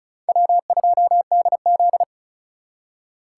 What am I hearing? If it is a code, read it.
Morse code, heard as a keyed tone, about 45 wpm, 700 Hz. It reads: W2DZ